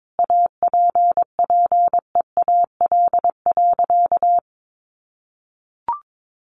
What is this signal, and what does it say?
Morse code, heard as a keyed tone, about 22 wpm, 700 Hz. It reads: APPEAL.
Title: Sentences from Top 1300 Words 22wpm